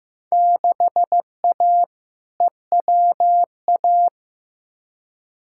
Morse code, heard as a keyed tone, about 16 wpm, 700 Hz. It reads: 6A EWA
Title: Field Day Exchanges Encore - 15wpm